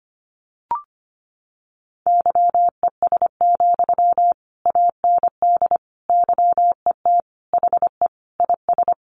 Morse code, heard as a keyed tone, about 25 wpm, 700 Hz. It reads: YES, AND YET HE IS
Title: Sentences from Top 700 Words 25wpm